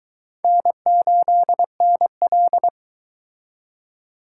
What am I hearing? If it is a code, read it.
Morse code, heard as a keyed tone, about 22 wpm, 700 Hz. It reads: N8NL